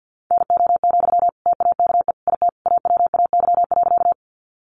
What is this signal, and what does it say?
Morse code, heard as a keyed tone, about 50 wpm, 700 Hz. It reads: NO, TAKE IT AWAY.
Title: Sentences from Top 600 Words 50wpm